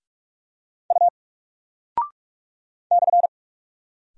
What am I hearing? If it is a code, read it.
Morse code, heard as a keyed tone, about 46 wpm, 700 Hz. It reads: U /
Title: Single Character (Letter:Number:Punctuation:Prosign) Alt 45wpm